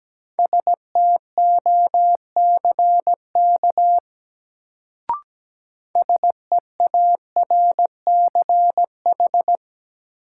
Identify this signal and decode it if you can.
Morse code, heard as a keyed tone, about 17 wpm, 700 Hz. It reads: STOCK SEARCH
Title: Sentences from Top 800 Words - New Word Drill - 17wpm